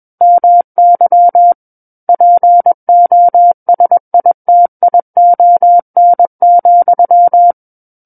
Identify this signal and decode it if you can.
Morse code, heard as a keyed tone, about 21 wpm, 700 Hz. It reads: MY POSITION,
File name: Book One / 12-Chapter 12 - 21 WPM